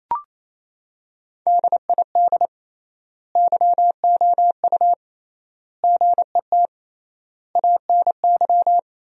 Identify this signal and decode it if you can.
Morse code, heard as a keyed tone, about 28 wpm, 700 Hz. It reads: DID YOU GET ANY
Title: 3x Word-Spacing - Sentences from Top 400 Words - 28wpm